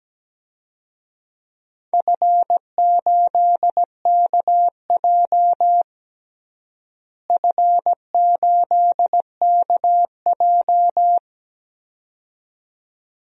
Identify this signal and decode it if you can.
Morse code, heard as a keyed tone, about 17 wpm, 700 Hz. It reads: F8KJ F8KJ
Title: DXCC-100 Call Signs - 4 Characters - Repeated 2x - 17wpm